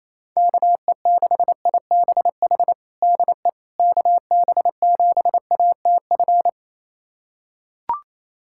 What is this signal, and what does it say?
Morse code, heard as a keyed tone, about 28 wpm, 700 Hz. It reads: KE6IBH DE KB7ATF
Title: USA Callsign DE Callsign - 6 Characters - 28wpm